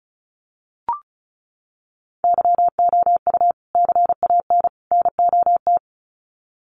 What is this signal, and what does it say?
Morse code, heard as a keyed tone, about 34 wpm, 700 Hz. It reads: YOU CAN NOT
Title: Sets of 3 Words 35wpm